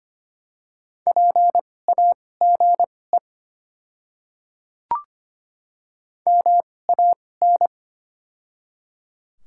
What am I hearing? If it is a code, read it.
Morse code, heard as a keyed tone, about 25 wpm, spaced as for 18 wpm, 700 Hz. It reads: PAGE MAN
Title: 2x Character Spacing - Top 300 Words - 25wpm